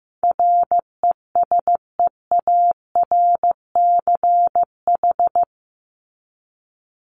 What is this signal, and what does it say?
Morse code, heard as a keyed tone, about 15 wpm, 700 Hz. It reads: RESEARCH
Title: Top 501-750 Words 15wpm